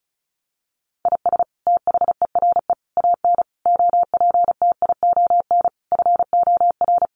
Morse code, heard as a keyed tone, about 34 wpm, 700 Hz. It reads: IS THERE AN OPTION FOR
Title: Sentences from Top 1300 Words 35wpm